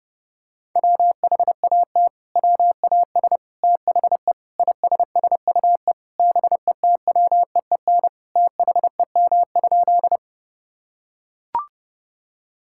Morse code, heard as a keyed tone, about 30 wpm, 700 Hz. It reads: WHAT WAS THE ISSUE BETWEEN THEM?
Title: Sentences from Top 1000 Words - Review Part 6 - 30wpm